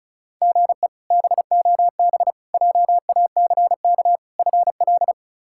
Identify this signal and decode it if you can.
Morse code, heard as a keyed tone, about 34 wpm, 700 Hz. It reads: GE BOB JACK FL
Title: K1USN SST QSO Practice - 35wpm